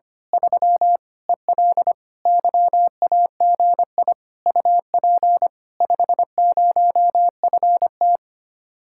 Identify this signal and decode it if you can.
Morse code, heard as a keyed tone, about 24 wpm, 700 Hz. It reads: 3 EL YAGI UP 50FT
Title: Speed-Racing Common QSO Phrases - 25wpm